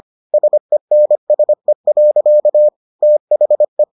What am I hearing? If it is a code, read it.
Morse code, heard as a keyed tone, about 25 wpm, 600 Hz. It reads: SENSE. THE